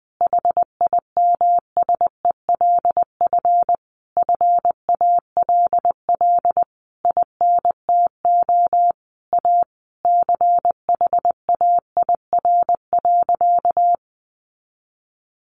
Morse code, heard as a keyed tone, about 20 wpm, 700 Hz. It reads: HIMSELF FALL INTO A CHAIR.
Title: A Sound Of Thunder 20wpm